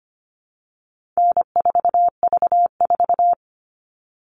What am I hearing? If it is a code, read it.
Morse code, heard as a keyed tone, about 25 wpm, 700 Hz. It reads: N4V4
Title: All Cut Numbers - 4-Digit 25wpm